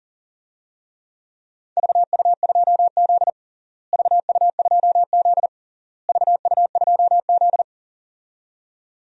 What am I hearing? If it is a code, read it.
Morse code, heard as a keyed tone, about 37 wpm, 700 Hz. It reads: VU2Z VU2Z VU2Z
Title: DXCC-100 Call Signs - 4 Characters - Repeated 3x - 40wpm